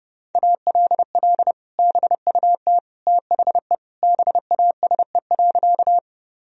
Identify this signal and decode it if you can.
Morse code, heard as a keyed tone, about 30 wpm, 700 Hz. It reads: ALL BUT THE BASE.
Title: Sentences from Top 1000 Words - Review Part 1 - 30wpm